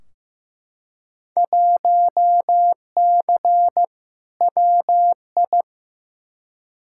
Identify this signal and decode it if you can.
Morse code, heard as a keyed tone, about 15 wpm, 700 Hz. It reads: 1C WI
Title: Field Day Exchanges - 15wpm